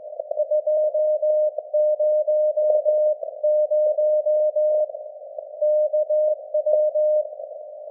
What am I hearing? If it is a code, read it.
Morse code, heard as a keyed tone, about 18 wpm, 610 Hz. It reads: 200 KW
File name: laengstwellenempfang / SDRSharp 17kHz AF